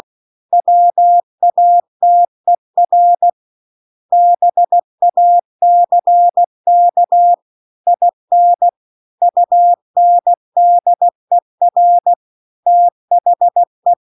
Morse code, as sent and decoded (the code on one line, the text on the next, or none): .-- .- - . .-. / -... .- -.-. -.- / .. -. / ..- -. -.. .-. / - .... .
WATER BACK IN UNDER THE